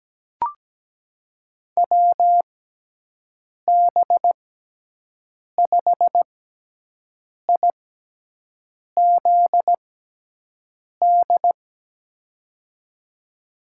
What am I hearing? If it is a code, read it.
Morse code, heard as a keyed tone, about 17 wpm, spaced as for 6 wpm, 700 Hz. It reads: WB5IZD
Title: USA Call Signs - 6x Character Spacing - 17wpm